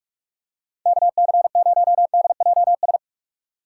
Morse code, heard as a keyed tone, about 41 wpm, 700 Hz. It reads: KC0DJS
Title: USA Call Signs - 6 Characters - 3x Character Spacing - 45wpm